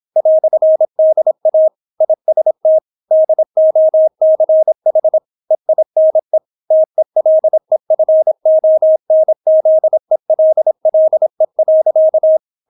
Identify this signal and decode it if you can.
Morse code, heard as a keyed tone, about 26 wpm, 630 Hz. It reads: "DA IST DOCH EINE TELEFONZELLE.